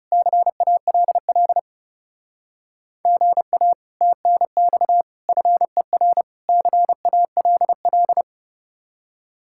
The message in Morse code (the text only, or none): CALL GA TNX FER CALL